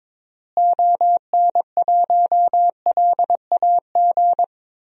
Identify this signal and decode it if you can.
Morse code, heard as a keyed tone, about 22 wpm, 700 Hz. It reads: ON1LAG
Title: DXCC-100 Call Signs - 6 Characters - Repeated 3x - 22wpm